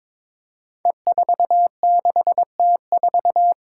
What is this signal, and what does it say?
Morse code, heard as a keyed tone, about 22 wpm, 700 Hz. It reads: E46T4
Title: Five Challenging Letter-Number Combinations 22wpm